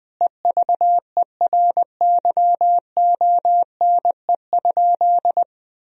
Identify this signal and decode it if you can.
Morse code, heard as a keyed tone, about 20 wpm, 700 Hz. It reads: EVERYONE?
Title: Sentences from Top 1000 Words - Review Part 4 - 20wpm